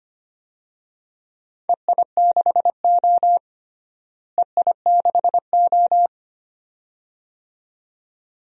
Morse code, heard as a keyed tone, about 25 wpm, 700 Hz. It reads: EI6O EI6O
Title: DXCC-100 Call Signs - 4 Characters - Repeated 2x - 25wpm